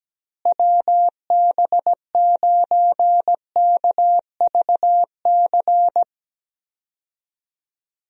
Morse code, heard as a keyed tone, about 17 wpm, 700 Hz. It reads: WB9KVC
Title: USA Call Signs - 6 Characters - Repeated 3x - 17wpm